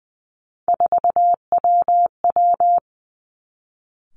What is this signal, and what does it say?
Morse code, heard as a keyed tone, about 20 wpm, 700 Hz. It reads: 4WW
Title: Three Challenging Letter-Number Combinations 20wpm